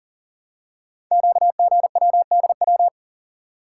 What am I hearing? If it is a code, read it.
Morse code, heard as a keyed tone, about 38 wpm, 700 Hz. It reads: QGWDW
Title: Code Groups - 5 Letters - 40wpm